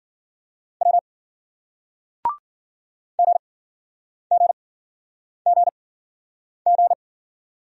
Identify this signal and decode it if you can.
Morse code, heard as a keyed tone, about 58 wpm, spaced as for 9 wpm, 700 Hz. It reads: W GGGG